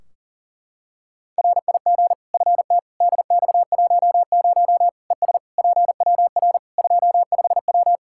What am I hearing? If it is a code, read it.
Morse code, heard as a keyed tone, about 40 wpm, 700 Hz. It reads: RIG FT DX10 ES PWR 25W
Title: Practice QSO - Exchange 2 - Info - 40wpm